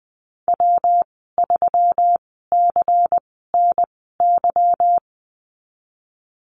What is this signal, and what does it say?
Morse code, heard as a keyed tone, about 19 wpm, spaced as for 14 wpm, 700 Hz. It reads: W3CNY
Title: 147 Lesson 24g - Letter y - Call Signs 20wpm